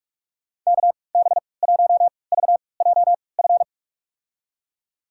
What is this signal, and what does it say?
Morse code, heard as a keyed tone, about 42 wpm, spaced as for 25 wpm, 700 Hz. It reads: KB1VJF
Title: USA Call Signs - 6 Characters - 3x Character Spacing - 45wpm